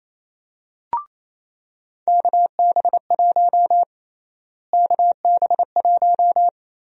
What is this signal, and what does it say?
Morse code, heard as a keyed tone, about 27 wpm, 700 Hz. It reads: KB1 KB1
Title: USA Call Sign Prefixes - Repeated 2x - 28wpm